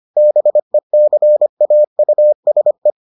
Morse code, heard as a keyed tone, about 25 wpm, 600 Hz. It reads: BECAUSE